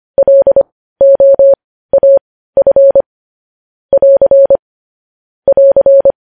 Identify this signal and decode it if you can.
Morse code, heard as a keyed tone, about 25 wpm, spaced as for 15 wpm, 550 Hz. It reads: LOAF <AR> <AR>
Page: News Headlines in Morse Code at 15 WPM